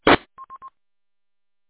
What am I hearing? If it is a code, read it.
There is no code.